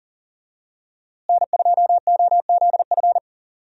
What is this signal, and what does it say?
Morse code, heard as a keyed tone, about 38 wpm, 700 Hz. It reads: N2OZF
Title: USA Call Signs - 5 Characters - Repeated 2x - 40wpm